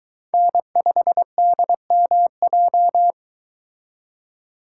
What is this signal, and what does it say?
Morse code, heard as a keyed tone, about 23 wpm, 700 Hz. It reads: N5DMJ